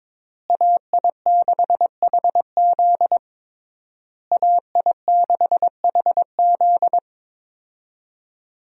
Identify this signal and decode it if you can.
Morse code, heard as a keyed tone, about 22 wpm, 700 Hz. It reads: AI6HZ AI6HZ
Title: USA Call Signs - 5 Characters - Repeated 3x - 22wpm